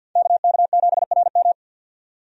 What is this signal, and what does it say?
Morse code, heard as a keyed tone, about 50 wpm, 700 Hz. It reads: KK7RM
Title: USA Call Signs - 5 Characters - 50wpm